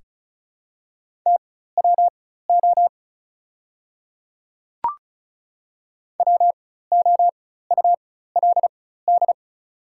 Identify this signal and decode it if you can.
Morse code, heard as a keyed tone, about 36 wpm, spaced as for 16 wpm, 690 Hz. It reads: TWO WOULD
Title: 4x Character Spacing - Top 100 Words - 35wpm